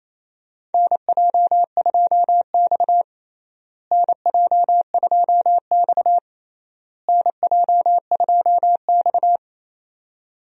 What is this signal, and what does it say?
Morse code, heard as a keyed tone, about 28 wpm, 700 Hz. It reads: NJ2X NJ2X NJ2X
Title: USA Call Signs - 4 Characters - Repeated 3x - 28wpm